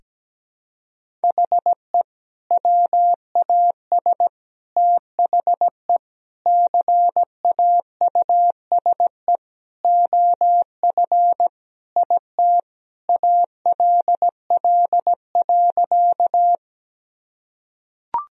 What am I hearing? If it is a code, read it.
Morse code, heard as a keyed tone, about 17 wpm, 700 Hz. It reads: HE WAS THE CAUSE OF IT ALL.
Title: Sentences from Top 1000 Words - Review Part 1 - 17wpm